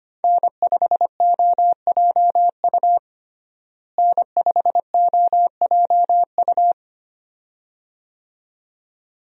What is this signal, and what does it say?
Morse code, heard as a keyed tone, about 25 wpm, 700 Hz. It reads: N5OJU N5OJU
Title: USA Call Signs - 5 Characters - Repeated 2x - 25wpm